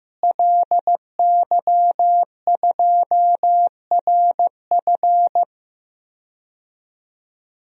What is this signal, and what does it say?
Morse code, heard as a keyed tone, about 15 wpm, 700 Hz. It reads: LY2RF
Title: DXCC-100 Call Signs - 5 Characters - Repeated 2x - 15wpm